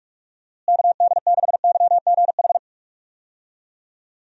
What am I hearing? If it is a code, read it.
Morse code, heard as a keyed tone, about 42 wpm, 700 Hz. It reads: KD6YGH
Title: USA Call Signs - 6 Characters - Repeated 2x - 45wpm